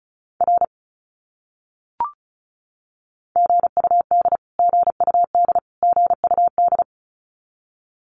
Morse code, heard as a keyed tone, about 34 wpm, 700 Hz. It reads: R GUD GUD GUD
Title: Abbreviations Repeated 3x - 35wpm